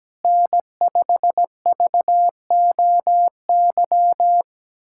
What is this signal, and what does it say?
Morse code, heard as a keyed tone, about 17 wpm, 700 Hz. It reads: N5VOY
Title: USA Callsign DE Callsign - 5 Characters - 17wpm